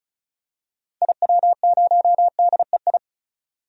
Morse code, heard as a keyed tone, about 36 wpm, 700 Hz. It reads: IW0DEI
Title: DXCC-100 Call Signs - 6 Characters - 35wpm